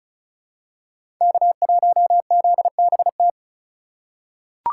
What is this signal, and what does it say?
Morse code, heard as a keyed tone, about 34 wpm, 700 Hz. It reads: K1ZBT
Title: USA Call Signs - 2x Character Spacing - 35wpm